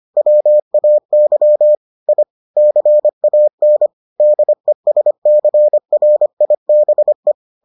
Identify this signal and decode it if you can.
Morse code, heard as a keyed tone, about 25 wpm, 600 Hz. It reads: WAY I CAN DESCRIBE